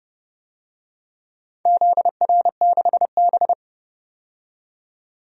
Morse code, heard as a keyed tone, about 29 wpm, 700 Hz. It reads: ZR6B
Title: DXCC-100 Call Signs - 4 Characters - 30wpm